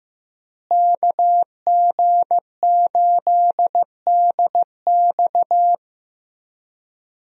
Morse code, heard as a keyed tone, about 16 wpm, 700 Hz. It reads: KG8DX
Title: USA Call Signs - 5 Characters - Repeated 2x - 15wpm